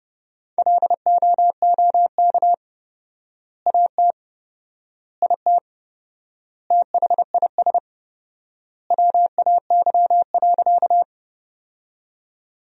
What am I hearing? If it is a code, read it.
Morse code, heard as a keyed tone, about 32 wpm, 700 Hz. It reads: LOOK AT IT THIS WAY.
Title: 4x Word-Spacing - Sentences from Top 100 Words - 30wpm